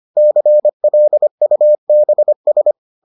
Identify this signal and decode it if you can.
Morse code, heard as a keyed tone, about 25 wpm, 600 Hz. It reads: CLUBS